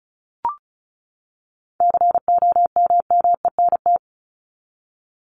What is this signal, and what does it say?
Morse code, heard as a keyed tone, about 34 wpm, 700 Hz. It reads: COMMENT
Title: Top 1751-2000 Words 35wpm